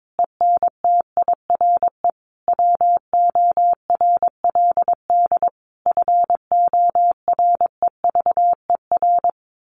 Morse code, heard as a keyed tone, about 22 wpm, 700 Hz. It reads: ENTIRE WORLD FOREVER
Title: A Sound Of Thunder 22wpm